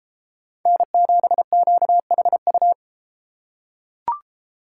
Morse code, heard as a keyed tone, about 32 wpm, 700 Hz. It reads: N7QHU